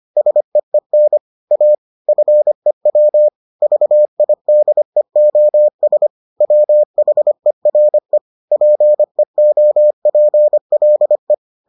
Morse code, heard as a keyed tone, about 25 wpm, 600 Hz. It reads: SEEN A FEW VIDEOS WHERE PEOPLE